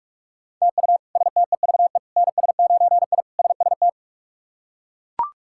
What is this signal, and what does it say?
Morse code, heard as a keyed tone, about 42 wpm, 700 Hz. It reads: TU STEVE NS9I SST
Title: K1USN SST QSO Practice - 45wpm